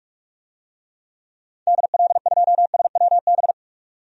Morse code, heard as a keyed tone, about 43 wpm, 700 Hz. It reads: DL2SWB
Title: DXCC-100 Call Signs - 6 Characters - 45wpm